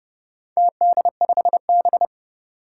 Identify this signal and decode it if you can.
Morse code, heard as a keyed tone, about 30 wpm, 700 Hz. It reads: TD5B
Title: Four Challenging Letter-Number Combinations 30wpm